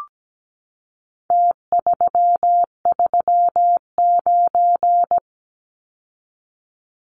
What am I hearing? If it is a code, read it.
Morse code, heard as a keyed tone, about 17 wpm, 700 Hz. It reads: T339
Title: Numbers - Contest Style - 4-Digit 17wpm